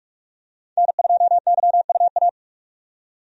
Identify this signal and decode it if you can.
Morse code, heard as a keyed tone, about 46 wpm, 700 Hz. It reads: N2YUA
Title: USA Call Signs - 5 Characters - 45wpm